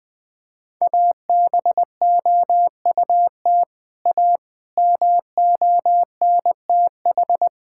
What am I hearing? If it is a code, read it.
Morse code, heard as a keyed tone, about 20 wpm, 700 Hz. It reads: ABOUT A MONTH
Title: Sentences from Top 1000 Words - Review Part 5 - 20wpm